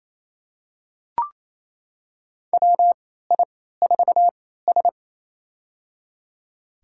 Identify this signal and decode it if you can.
Morse code, heard as a keyed tone, about 28 wpm, spaced as for 16 wpm, 700 Hz. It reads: WI4S
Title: USA Call Signs - 4 Characters - 3x Character Spacing - 28wpm